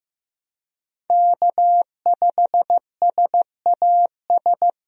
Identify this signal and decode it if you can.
Morse code, heard as a keyed tone, about 15 wpm, 700 Hz. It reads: K5SAS
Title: USA Call Signs - 5 Characters - Repeated 3x - 15wpm